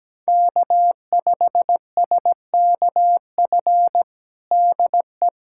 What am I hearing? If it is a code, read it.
Morse code, heard as a keyed tone, about 17 wpm, 700 Hz. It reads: K5SKF DE
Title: USA Callsign DE Callsign - 5 Characters - 17wpm